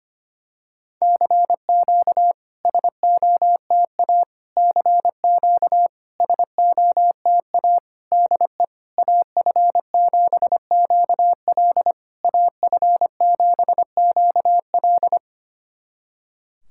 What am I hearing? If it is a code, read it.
Morse code, heard as a keyed tone, about 25 wpm, 690 Hz. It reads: CQ SOTA CQ SOTA DE AF7QL AF7QL
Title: SOTA QSO Practice 25wpm